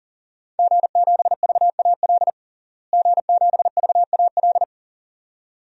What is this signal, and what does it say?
Morse code, heard as a keyed tone, about 37 wpm, 700 Hz. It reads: G7VAL G7VAL
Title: DXCC-100 Call Signs - 5 Characters - Repeated 2x - 40wpm